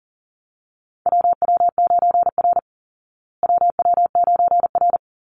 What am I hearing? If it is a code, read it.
Morse code, heard as a keyed tone, about 40 wpm, 700 Hz. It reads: WW9R WW9R
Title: Call Signs Repeated 3x - 40wpm